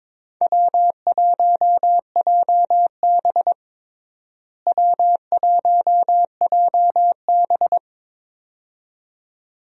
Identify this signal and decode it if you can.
Morse code, heard as a keyed tone, about 22 wpm, 700 Hz. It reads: W1JB W1JB
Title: USA Call Signs - 4 Characters - Repeated 2x - 22wpm